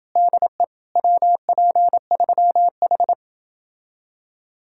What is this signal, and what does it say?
Morse code, heard as a keyed tone, about 30 wpm, 700 Hz. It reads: DE WP3H